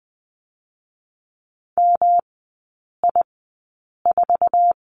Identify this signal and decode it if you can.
Morse code, heard as a keyed tone, about 20 wpm, spaced as for 8 wpm, 700 Hz. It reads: MI4
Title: 184 Lesson 30e - Letter b - Three Characters 20wpm